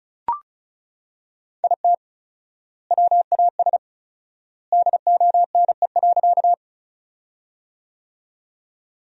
Morse code, heard as a keyed tone, about 36 wpm, 700 Hz. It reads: IT WAS DONE.
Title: 4x Word-Spacing - Sentences from Top 500 Words - 35wpm